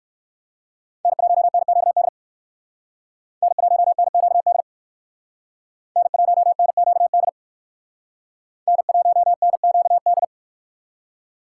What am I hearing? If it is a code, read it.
Morse code, heard as a keyed tone, about 57 wpm, 700 Hz. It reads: N1NQD N1NQD N1NQD N1NQD